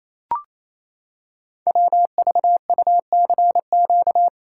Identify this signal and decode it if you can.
Morse code, heard as a keyed tone, about 27 wpm, 700 Hz. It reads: WVUCQ
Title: Code Groups - 5 Letters - 28wpm